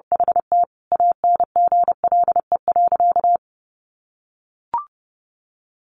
Morse code, heard as a keyed tone, about 30 wpm, 700 Hz. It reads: HT ANGLE.
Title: Sentences from Top 1500 Words 30wpm